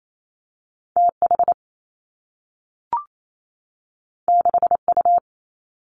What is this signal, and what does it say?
Morse code, heard as a keyed tone, about 28 wpm, 700 Hz. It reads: T H 6 U
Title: Two Challenging Letter-Number Combinations 28wpm